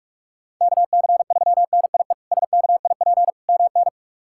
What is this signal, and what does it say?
Morse code, heard as a keyed tone, about 46 wpm, 700 Hz. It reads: KC3NIE SKIP MN
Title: K1USN SST QSO Practice - 45wpm